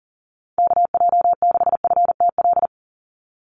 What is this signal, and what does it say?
Morse code, heard as a keyed tone, about 37 wpm, 690 Hz. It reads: KJ6FTL
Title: USA Call Signs 40wpm